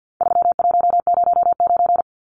Morse code, heard as a keyed tone, about 50 wpm, 700 Hz. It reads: 3109
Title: Numbers - 4-Digit 50wpm